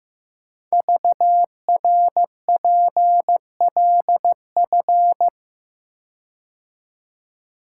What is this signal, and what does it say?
Morse code, heard as a keyed tone, about 15 wpm, 700 Hz. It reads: VRPLF
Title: Code Groups - 5 Letters - 15wpm